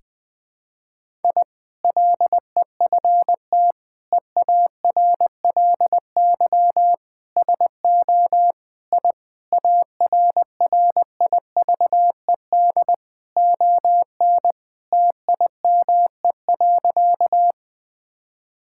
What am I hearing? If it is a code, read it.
Morse code, heard as a keyed tone, about 21 wpm, 700 Hz. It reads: I LEFT EARLY SO I ARRIVED ON TIME.
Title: Sentences with 2-Word Phrases - Spoken Beginning - 20wpm